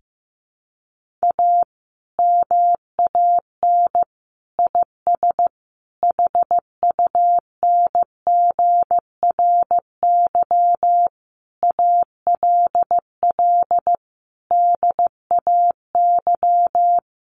Morse code, heard as a keyed tone, about 15 wpm, 700 Hz. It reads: A MAN IS HUNGRY ALL DAY
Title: Sentences from Top 1400 Words 15wpm